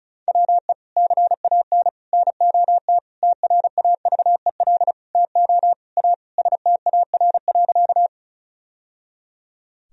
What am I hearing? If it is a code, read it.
Morse code, heard as a keyed tone, about 36 wpm, 700 Hz. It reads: WE CAN NOT TRAVEL TO A STAR.
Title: Sentences from Top 1000 Words - Review Part 2 - 35wpm